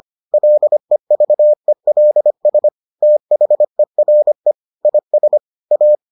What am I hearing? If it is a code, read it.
Morse code, heard as a keyed tone, about 25 wpm, 600 Hz. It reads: LEVELS THERE IS A